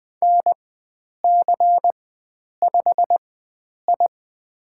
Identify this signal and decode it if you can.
Morse code, heard as a keyed tone, about 20 wpm, spaced as for 9 wpm, 700 Hz. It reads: NC5I
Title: USA Call Signs - 4x Character Spacing - 20wpm